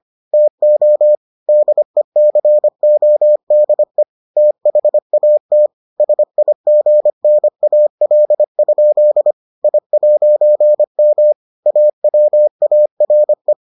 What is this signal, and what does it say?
Morse code, heard as a keyed tone, about 25 wpm, 600 Hz. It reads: TO DECODE THAT SIGNAL? I'M AWARE